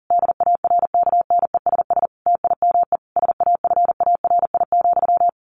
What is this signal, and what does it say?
Morse code, heard as a keyed tone, about 37 wpm, 700 Hz. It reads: DARKNESS TIME SAFARI,
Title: A Sound Of Thunder 40wpm